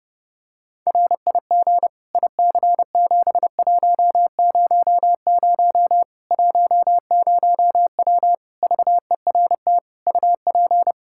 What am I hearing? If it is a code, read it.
Morse code, heard as a keyed tone, about 30 wpm, 700 Hz. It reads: RIG IC7100 10W VERT UP
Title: Practice QSO - Exchange 2 - Info - 30wpm